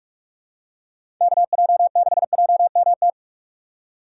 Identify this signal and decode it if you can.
Morse code, heard as a keyed tone, about 44 wpm, 700 Hz. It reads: KJ6JMT
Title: USA Call Signs - 6 Characters - 2x Character Spacing - 45wpm